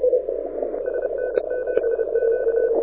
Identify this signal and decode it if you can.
Morse code, heard as a keyed tone, about 30 wpm, 1500 Hz, with radio noise. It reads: SN7Q
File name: audio zawody / IARU 2015 CW SN7Q